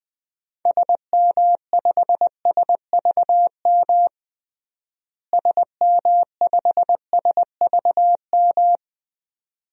Morse code, heard as a keyed tone, about 21 wpm, 700 Hz. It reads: SM5SVM SM5SVM
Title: DXCC-100 Call Signs - 6 Characters - Repeated 2x - 20wpm